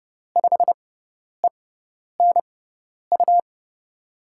Morse code, heard as a keyed tone, about 30 wpm, spaced as for 10 wpm, 700 Hz. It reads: 5ENU
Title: USA Call Signs - 6x Character Spacing - 30wpm